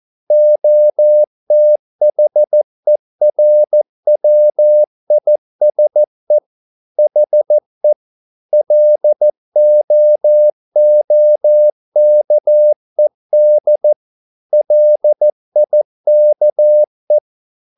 Morse code, heard as a keyed tone, about 14 wpm, 600 Hz. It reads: OTHERWISE HE LOOKED LIKE